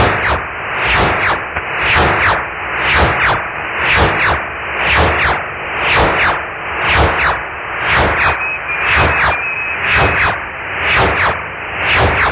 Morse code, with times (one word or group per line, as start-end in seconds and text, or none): none